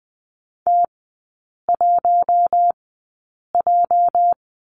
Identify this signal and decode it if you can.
Morse code, heard as a keyed tone, about 20 wpm, spaced as for 8 wpm, 700 Hz. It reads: T1J
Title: 218 Lesson 35e - Number 0 - Three Characters 20wpm